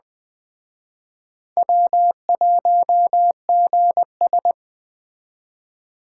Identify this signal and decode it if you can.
Morse code, heard as a keyed tone, about 21 wpm, 700 Hz. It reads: W1GS
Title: Speed Racing - USA 1x2 Call Signs 17wpm